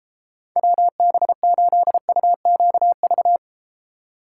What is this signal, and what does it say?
Morse code, heard as a keyed tone, about 34 wpm, 700 Hz. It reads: WB8UQV